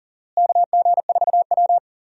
Morse code, heard as a keyed tone, about 37 wpm, 700 Hz. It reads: KG4W
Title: USA Call Signs - 4 Characters - Repeated 3x - 40wpm